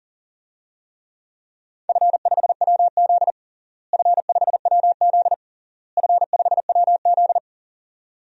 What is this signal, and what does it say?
Morse code, heard as a keyed tone, about 40 wpm, 700 Hz. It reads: F5WZ F5WZ F5WZ
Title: DXCC-100 Call Signs - 4 Characters - Repeated 3x - 40wpm